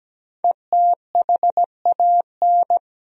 Morse code, heard as a keyed tone, about 17 wpm, 700 Hz. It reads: ETHAN
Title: USA Common Names - Max 5 Letters - 17wpm